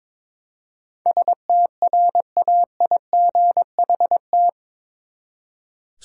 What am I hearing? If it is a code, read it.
Morse code, heard as a keyed tone, about 21 wpm, 700 Hz. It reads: STRAIGHT
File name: practice / 22wpm / Eight Letter Words - 22wpm